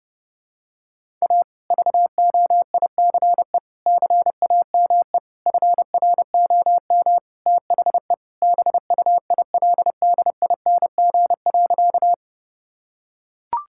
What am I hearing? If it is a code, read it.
Morse code, heard as a keyed tone, about 30 wpm, 700 Hz. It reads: A VOICE CAME FROM THE BUILDING.
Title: Sentences from Top 1000 Words - Review Part 5 - 30wpm